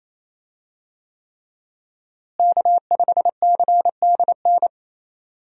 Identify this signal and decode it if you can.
Morse code, heard as a keyed tone, about 27 wpm, 700 Hz. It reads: K5CDN
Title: USA Call Signs - 5 Characters - Repeated 3x - 28wpm